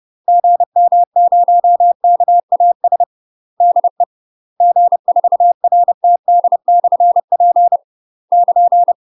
Morse code, heard as a keyed tone, about 30 wpm, 700 Hz. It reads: GM0KAS DE G4RTD/P <KN>
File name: UK MASTER QSOs TRAINING / 30WPM / UK MASTER QSOs 0050